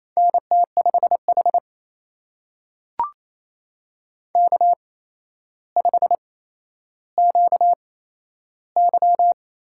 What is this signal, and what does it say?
Morse code, heard as a keyed tone, about 27 wpm, spaced as for 12 wpm, 700 Hz. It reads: NT5H K5QY
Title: USA Call Signs - 8x Character Spacing - 28wpm